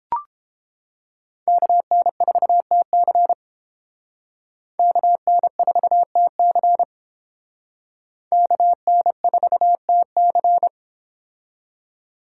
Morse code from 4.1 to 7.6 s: KN4TC

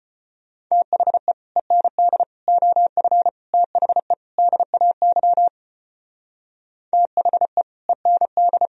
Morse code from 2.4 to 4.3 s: OF THE

